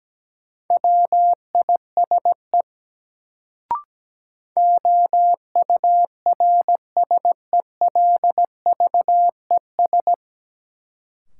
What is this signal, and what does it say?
Morse code, heard as a keyed tone, about 17 wpm, 700 Hz. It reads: WISE OURSELVES
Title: Sentences from Top 1600 Words - New Word Drill - 17wpm